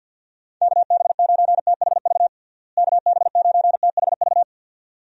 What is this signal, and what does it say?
Morse code, heard as a keyed tone, about 47 wpm, 700 Hz. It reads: KB9THV KB9THV
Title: USA Call Signs - 6 Characters - Repeated 2x - 50wpm